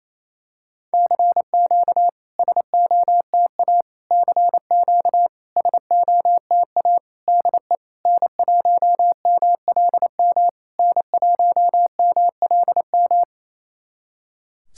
Morse code, heard as a keyed tone, about 28 wpm, 700 Hz. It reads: CQ SOTA CQ SOTA DE N1MLM N1MLM
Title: SOTA QSO Practice 28wpm